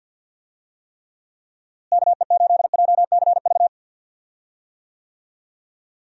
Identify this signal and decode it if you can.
Morse code, heard as a keyed tone, about 51 wpm, 700 Hz. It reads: KE8JCV
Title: USA Call Signs - 6 Characters - Repeated 2x - 50wpm